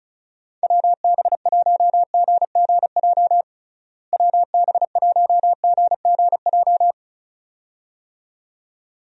Morse code, heard as a keyed tone, about 36 wpm, 700 Hz. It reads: WB1GGJ WB1GGJ
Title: USA Call Signs - 6 Characters - Repeated 2x - 35wpm